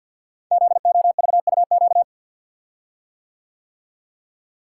Morse code, heard as a keyed tone, about 46 wpm, 700 Hz. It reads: ZOVUQ
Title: Code Groups - 5 Letters - 50wpm